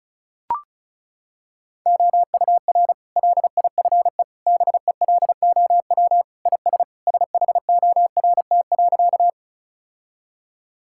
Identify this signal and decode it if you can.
Morse code, heard as a keyed tone, about 34 wpm, 700 Hz. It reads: OUR LIFE BELOW IS SHORT.
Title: Sentences from Top 1000 Words - Review Part 5 - 35wpm